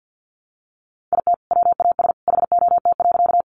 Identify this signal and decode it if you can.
Morse code, heard as a keyed tone, about 50 wpm, 700 Hz. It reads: IT WAS HOT.
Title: Sentences from Top 500 Words 50wpm